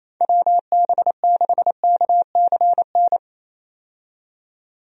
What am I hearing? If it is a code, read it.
Morse code, heard as a keyed tone, about 27 wpm, 700 Hz. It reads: WB6KCN
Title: USA Call Signs - 6 Characters - 28wpm